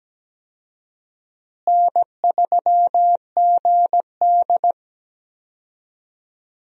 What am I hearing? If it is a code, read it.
Morse code, heard as a keyed tone, about 17 wpm, 700 Hz. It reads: N3GD
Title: USA Call Signs - 4 Characters - Repeated 3x - 17wpm